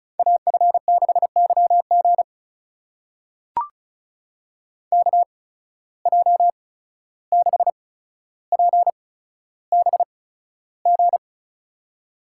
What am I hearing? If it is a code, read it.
Morse code, heard as a keyed tone, about 36 wpm, spaced as for 14 wpm, 700 Hz. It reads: AF6YG KJ6PBG
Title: USA Call Signs - 8x Character Spacing - 35wpm